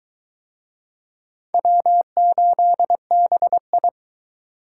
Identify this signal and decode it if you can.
Morse code, heard as a keyed tone, about 20 wpm, 700 Hz. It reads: W8BI